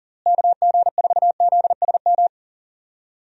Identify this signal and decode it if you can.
Morse code, heard as a keyed tone, about 40 wpm, 700 Hz. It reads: KG4ZSM
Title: USA Call Signs - 6 Characters - Repeated 2x - 40wpm